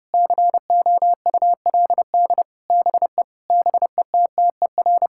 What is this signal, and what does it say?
Morse code, heard as a keyed tone, about 32 wpm, 700 Hz. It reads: COULD BE BETTER
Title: Sentences from Top 1000 Words - Review Part 6 - 30wpm